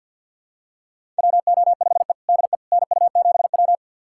Morse code, heard as a keyed tone, about 51 wpm, 700 Hz. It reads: WO5E DE NU7W
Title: USA Callsign DE Callsign - 4 Characters - 50wpm